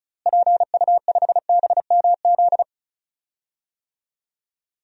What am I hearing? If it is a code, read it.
Morse code, heard as a keyed tone, about 36 wpm, 700 Hz. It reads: PU5BMZ
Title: DXCC-100 Call Signs - 6 Characters - Repeated 3x - 35wpm